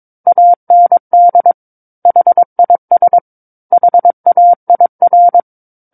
Morse code, heard as a keyed tone, about 22 wpm, 700 Hz. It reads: AND HIS HAIR
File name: Book One / 13-Chapter 13 - 22 WPM